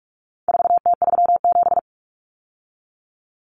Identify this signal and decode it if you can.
Morse code, heard as a keyed tone, about 44 wpm, 700 Hz. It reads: 4T37
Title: Common Cut Numbers - 4-Digit 45wpm